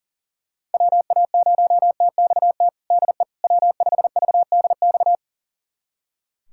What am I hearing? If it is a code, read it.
Morse code, heard as a keyed tone, about 40 wpm, 700 Hz. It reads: WA0TXT DE W5VDX
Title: USA Callsign DE Callsign 40wpm